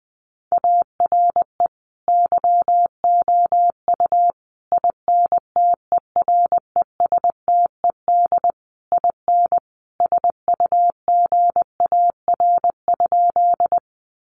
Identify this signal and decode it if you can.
Morse code, heard as a keyed tone, about 20 wpm, 700 Hz. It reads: ARE YOU INTERESTED IN SUGAR?
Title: Sentences from Top 1200 Words 20wpm